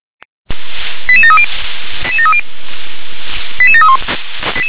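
Five-tone selective call (ZVEI), heard as quick runs of five tones, tones of about 70 ms. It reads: 88730 88731
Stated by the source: ZVEI five-tone call received on an SDR.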